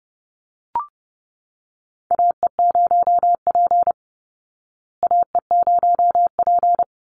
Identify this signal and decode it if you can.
Morse code, heard as a keyed tone, about 30 wpm, 700 Hz. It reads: AE0P AE0P
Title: Call Signs Repeated 3x - 30wpm